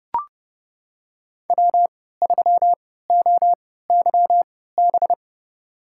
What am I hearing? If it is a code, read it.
Morse code, heard as a keyed tone, about 30 wpm, spaced as for 17 wpm, 700 Hz. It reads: W3OYB
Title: USA Call Signs - 5 Characters - 3x Character Spacing - 30wpm